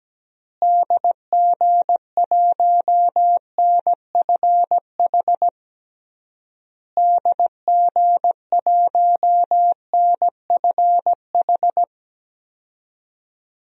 Morse code, heard as a keyed tone, about 17 wpm, 700 Hz. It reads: DG1NFH DG1NFH
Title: DXCC-100 Call Signs - 6 Characters - Repeated 3x - 17wpm